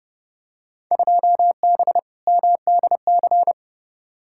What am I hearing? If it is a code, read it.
Morse code, heard as a keyed tone, about 29 wpm, 700 Hz. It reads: 2B MDC
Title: Field Day Exchanges - 30wpm